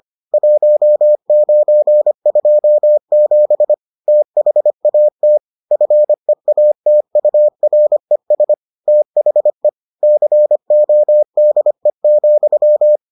Morse code, heard as a keyed tone, about 25 wpm, 600 Hz. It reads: 1927 THAT FEATURES THE CODE,